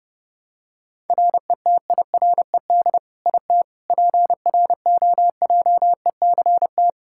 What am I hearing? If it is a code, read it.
Morse code, heard as a keyed tone, about 32 wpm, 700 Hz. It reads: RETIRED IT PROJECT
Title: Practice QSO - Exchange 3 - More Info - 30wpm